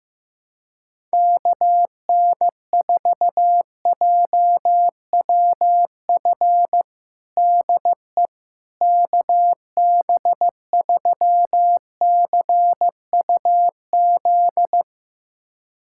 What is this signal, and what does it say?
Morse code, heard as a keyed tone, about 16 wpm, 700 Hz. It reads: KN4JWF DE KB3CUZ
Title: USA Callsign DE Callsign 15wpm